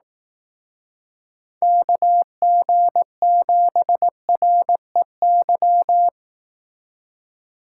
Morse code, heard as a keyed tone, about 18 wpm, 700 Hz. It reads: KG7REY
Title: Speed Racing - USA 2x3 Call Signs 15wpm